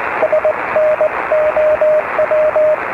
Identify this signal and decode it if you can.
Morse code, heard as a keyed tone, about 20 wpm, 620 Hz, with radio noise. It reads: SNOW